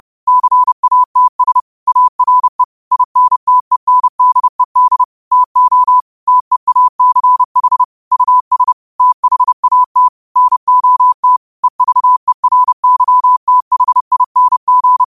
Morse code, heard as a keyed tone, about 30 wpm, 1000 Hz. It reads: CATS ARE INTENDED TO TEACH US THAT NOT EVERYTHING
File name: qotdfiles / 30wpm